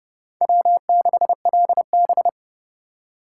Morse code, heard as a keyed tone, about 38 wpm, 700 Hz. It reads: W6LB